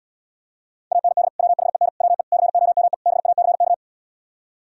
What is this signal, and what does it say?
Morse code, heard as a keyed tone, about 78 wpm, 700 Hz. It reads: AND WHEN WE COME BACK